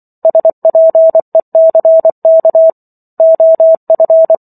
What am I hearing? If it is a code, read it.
Morse code, heard as a keyed tone, about 24 wpm, 650 Hz. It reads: SPECK OF
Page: War of the Worlds - 15-Chapter 15 - 24 WPM